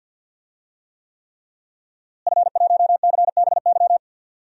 Morse code, heard as a keyed tone, about 53 wpm, 700 Hz. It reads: F1CBY
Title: DXCC-100 Call Signs - 5 Characters - Repeated 3x - 50wpm